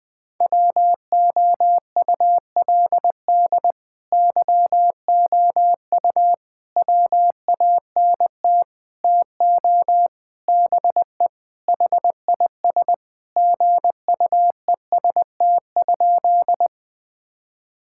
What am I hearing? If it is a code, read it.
Morse code, heard as a keyed tone, about 21 wpm, 700 Hz. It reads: WOULD YOU WANT TO BE HIS GUEST?
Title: Sentences from Top 1600 Words 20wpm